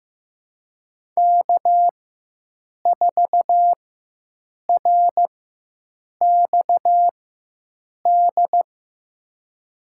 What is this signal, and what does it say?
Morse code, heard as a keyed tone, about 15 wpm, spaced as for 7 wpm, 700 Hz. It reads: K4RXD
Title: USA Call Signs - 5 Characters - 4x Character Spacing - 15wpm